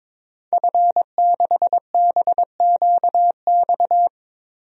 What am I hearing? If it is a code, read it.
Morse code, heard as a keyed tone, about 22 wpm, 700 Hz. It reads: F6BQX
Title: DXCC-100 Call Signs - 5 Characters - 22wpm